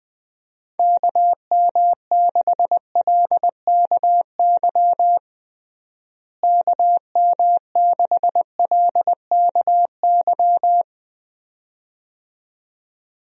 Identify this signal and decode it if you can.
Morse code, heard as a keyed tone, about 20 wpm, 700 Hz. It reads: KM6LKY KM6LKY
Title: USA Call Signs - 6 Characters - Repeated 2x - 20wpm